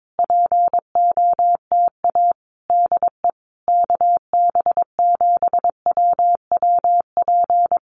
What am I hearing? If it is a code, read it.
Morse code, heard as a keyed tone, about 22 wpm, 700 Hz. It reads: POTA DE KB7WWP
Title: Common QSO Phrases Encore 22wpm